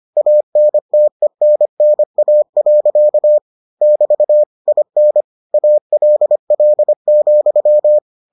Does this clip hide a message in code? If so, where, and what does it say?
Yes: Morse code from 0.2 to 8.0 s, heard as a keyed tone, about 25 wpm, 600 Hz. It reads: ANTENNA. <BT> IN ALL,